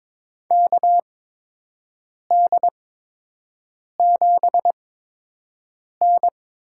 Morse code, heard as a keyed tone, about 22 wpm, spaced as for 6 wpm, 700 Hz. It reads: KD7N
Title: USA Call Signs - 6 Characters - 8x Character Spacing - 22wpm